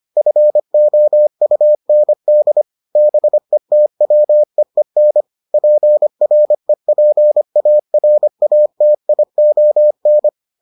Morse code, heard as a keyed tone, about 25 wpm, 600 Hz. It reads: FOUND BETWEEN PREPARATION